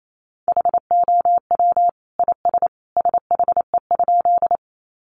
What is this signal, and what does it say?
Morse code, heard as a keyed tone, about 28 wpm, 700 Hz. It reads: HOW IS SHE?
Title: Sentences from Top 100 Words 28wpm